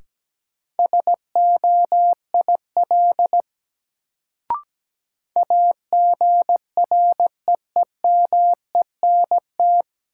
Morse code, heard as a keyed tone, about 17 wpm, 700 Hz. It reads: SOIL AGREEMENT
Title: Sentences from Top 1500 Words - New Word Drill Alt - 17wpm